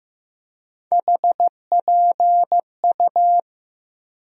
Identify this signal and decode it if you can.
Morse code, heard as a keyed tone, about 15 wpm, 700 Hz. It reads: HPU
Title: USA Call Sign Postfixes - 3 Letters 15wpm